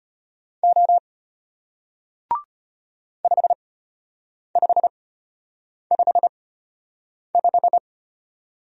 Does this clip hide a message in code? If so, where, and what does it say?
Yes: Morse code from 0.6 to 7.8 s, heard as a keyed tone, about 33 wpm, spaced as for 8 wpm, 700 Hz. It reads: O 5555